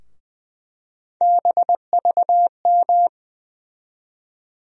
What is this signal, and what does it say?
Morse code, heard as a keyed tone, about 21 wpm, 700 Hz. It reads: BVM
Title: USA Call Sign Postfixes - 3 Letters 20wpm